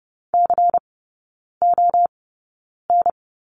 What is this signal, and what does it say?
Morse code, heard as a keyed tone, about 30 wpm, spaced as for 9 wpm, 690 Hz. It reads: CON
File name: Lesson 17e - Letter u - Three Characters 30wpm